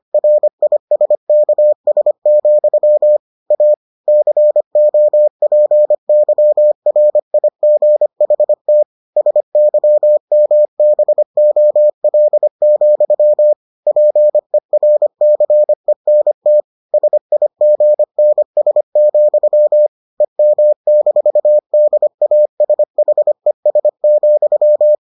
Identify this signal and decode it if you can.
Morse code, heard as a keyed tone, about 25 wpm, 600 Hz. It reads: RISKS, A COPYRIGHT SYMBOL, PERCENT SIGNS, EM-DASHES,